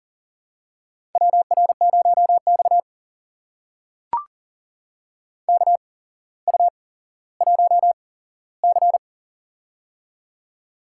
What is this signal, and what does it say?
Morse code, heard as a keyed tone, about 40 wpm, spaced as for 17 wpm, 700 Hz. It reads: WR0X KU1C
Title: USA Call Signs - 4 Characters - 8x Character Spacing - 40wpm